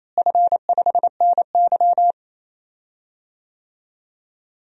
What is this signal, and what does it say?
Morse code, heard as a keyed tone, about 27 wpm, 700 Hz. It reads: F5NY
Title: DXCC-100 Call Signs - 4 Characters - Repeated 2x - 28wpm